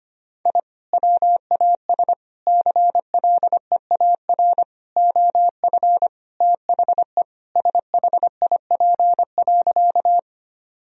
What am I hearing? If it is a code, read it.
Morse code, heard as a keyed tone, about 25 wpm, 700 Hz. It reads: I WAS CLEAR OF THE SHIP.
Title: Sentences from Top 1000 Words - Review Part 2 - 25wpm